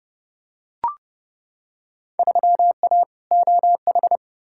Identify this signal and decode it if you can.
Morse code, heard as a keyed tone, about 30 wpm, 700 Hz. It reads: 3A OH
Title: Field Day Exchanges - 30wpm